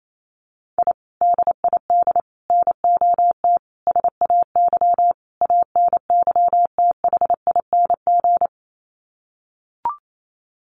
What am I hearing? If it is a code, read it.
Morse code, heard as a keyed tone, about 28 wpm, 700 Hz. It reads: I DID NOT SAY ANYTHING
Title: Sets of 5 Words Encore2 28wpm